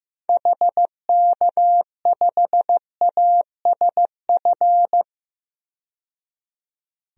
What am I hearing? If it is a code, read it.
Morse code, heard as a keyed tone, about 15 wpm, 700 Hz. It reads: HK5ASF
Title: DXCC-100 Call Signs - 6 Characters - 15wpm